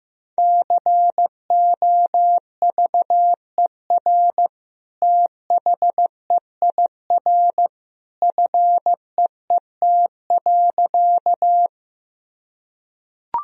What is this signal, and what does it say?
Morse code, heard as a keyed tone, about 15 wpm, 700 Hz. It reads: COVER THEIR FEET.
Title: Sentences from Top 1000 Words - Review Part 1 - 15wpm